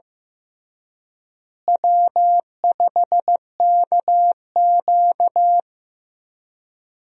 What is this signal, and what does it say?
Morse code, heard as a keyed tone, about 15 wpm, 700 Hz. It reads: W5KQ
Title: USA Call Signs - 4 Characters - Repeated 3x - 15wpm